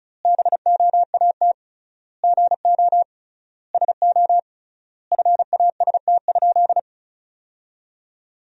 Morse code, heard as a keyed tone, about 36 wpm, 700 Hz. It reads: BOAT GO SO FAST?
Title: 3x Word-Spacing - Sentences from Top 500 Words - 35wpm